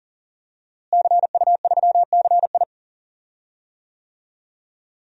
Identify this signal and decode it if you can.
Morse code, heard as a keyed tone, about 40 wpm, 700 Hz. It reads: CU3CI
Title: DXCC-100 Call Signs - 5 Characters - Repeated 2x - 40wpm